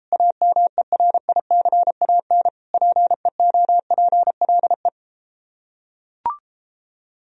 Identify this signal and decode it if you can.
Morse code, heard as a keyed tone, about 33 wpm, 700 Hz. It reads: AMERICAN PEOPLE